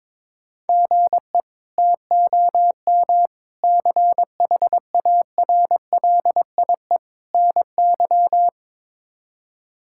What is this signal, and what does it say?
Morse code, heard as a keyed tone, about 22 wpm, 700 Hz. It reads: GE TOM CHARLIE NY
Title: K1USN SST QSO Practice - 22wpm